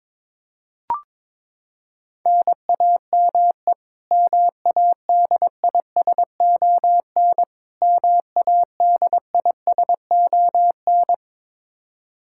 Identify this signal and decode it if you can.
Morse code, heard as a keyed tone, about 22 wpm, 700 Hz. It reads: NAME MADISON MADISON
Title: K1USN SST QSO Practice 22wpm